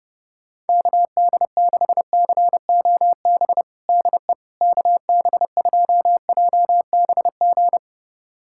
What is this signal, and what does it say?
Morse code, heard as a keyed tone, about 29 wpm, 700 Hz. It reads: KD6COB DE KB2JBG
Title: USA Callsign DE Callsign - 6 Characters - 30wpm